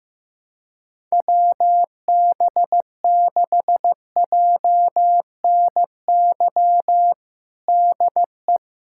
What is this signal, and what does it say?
Morse code, heard as a keyed tone, about 15 wpm, 700 Hz. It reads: WB6JNY DE
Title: USA Callsign DE Callsign 15wpm